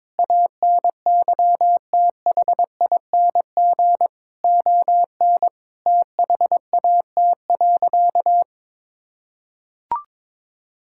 Morse code, heard as a keyed tone, about 22 wpm, 700 Hz. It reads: ANYTHING ON THAT.
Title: Sentences from Top 1000 Words - Review Part 5 - 22wpm